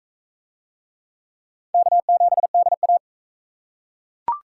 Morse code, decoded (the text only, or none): K7DA